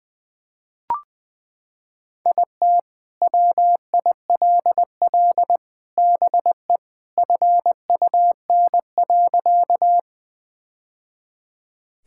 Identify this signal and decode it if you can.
Morse code, heard as a keyed tone, about 20 wpm, 700 Hz. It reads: IT WILL BE FUN.
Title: Sentences with 3-Word Phrases - 20wpm